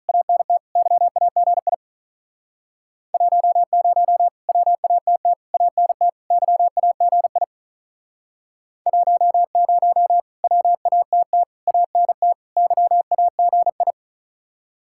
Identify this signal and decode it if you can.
Morse code, heard as a keyed tone, about 47 wpm, 700 Hz. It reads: ANT YAGI 10 WATT ANT YAGI 10 WATT ANT YAGI